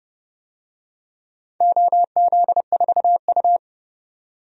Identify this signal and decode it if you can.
Morse code, heard as a keyed tone, about 30 wpm, 700 Hz. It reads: OZ4U
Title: DXCC-100 Call Signs - 4 Characters - 30wpm